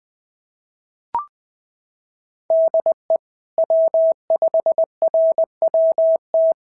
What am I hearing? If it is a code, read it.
Morse code, heard as a keyed tone, about 19 wpm, 650 Hz. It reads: DE W5RWT
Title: SOTA QSO Practice 20wpm